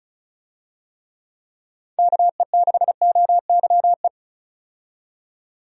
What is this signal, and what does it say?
Morse code, heard as a keyed tone, about 34 wpm, 700 Hz. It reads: KE6OYE
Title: USA Call Signs - 6 Characters - 35wpm